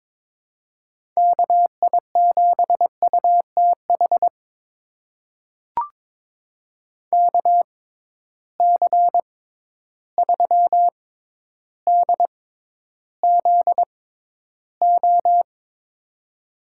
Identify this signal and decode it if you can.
Morse code, heard as a keyed tone, about 22 wpm, spaced as for 11 wpm, 700 Hz. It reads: KI7UTH KC3DZO
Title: USA Call Signs - 6 Characters - 6x Character Spacing - 22wpm